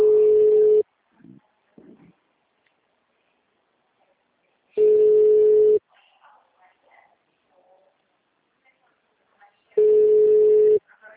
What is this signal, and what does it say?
Ringback tone (Europe, 425 Hz), with speech.